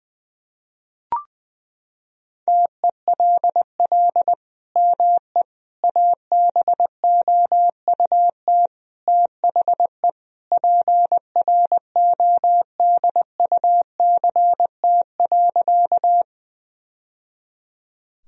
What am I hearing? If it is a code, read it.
Morse code, heard as a keyed tone, about 20 wpm, 700 Hz. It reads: TELL ME ABOUT THE PRODUCT.
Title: Sentences from Top 1000 Words - Review Part 2 - 20wpm